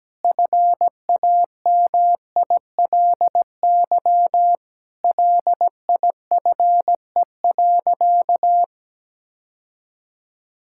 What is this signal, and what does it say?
Morse code, heard as a keyed tone, about 17 wpm, 700 Hz. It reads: FAMILY LIFE.
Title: Sentences from Top 1000 Words - Review Part 5 - 17wpm